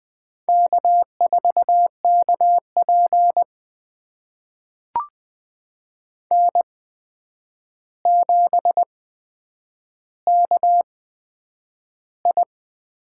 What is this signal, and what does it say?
Morse code, heard as a keyed tone, about 21 wpm, spaced as for 9 wpm, 700 Hz. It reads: K4KP N7KI
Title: USA Call Signs - 4 Characters - 8x Character Spacing - 20wpm